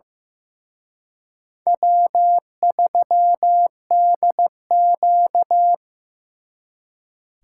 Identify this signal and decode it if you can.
Morse code, heard as a keyed tone, about 18 wpm, 700 Hz. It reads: W3DQ
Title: Speed Racing - USA 1x2 Call Signs 15wpm